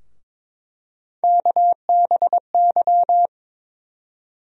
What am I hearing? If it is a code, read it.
Morse code, heard as a keyed tone, about 22 wpm, 700 Hz. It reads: KBY
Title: USA Call Sign Postfixes - 3 Letters 22wpm